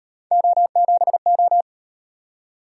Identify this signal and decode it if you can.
Morse code, heard as a keyed tone, about 38 wpm, 700 Hz. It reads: O7O